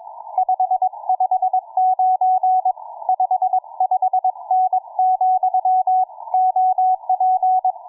Morse code, heard as a keyed tone, about 22 wpm, 740 Hz. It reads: 559 55N, OP